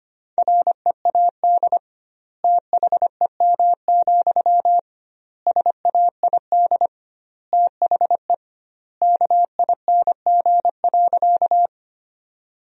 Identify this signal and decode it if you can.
Morse code, heard as a keyed tone, about 25 wpm, 700 Hz. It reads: READ THEM, SAID THE KING.
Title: 2x Word-Spacing - Sentences from Top 400 Words - 25wpm